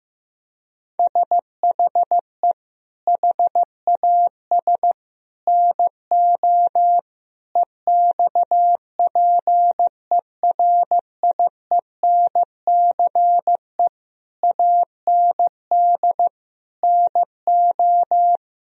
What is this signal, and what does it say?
Morse code, heard as a keyed tone, about 16 wpm, 700 Hz. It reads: SHE HAS NO EXPERIENCE AND NO
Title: Sentences from Top 1000 Words - Review Part 4 - 15wpm